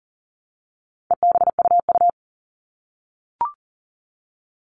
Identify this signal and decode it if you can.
Morse code, heard as a keyed tone, about 40 wpm, 700 Hz. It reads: EBUU
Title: All Cut Numbers - 4-Digit 40wpm